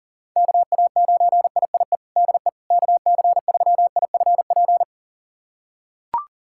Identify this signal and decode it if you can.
Morse code, heard as a keyed tone, about 40 wpm, 700 Hz. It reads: KA9IIE DE KC3IFP
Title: USA Callsign DE Callsign 40wpm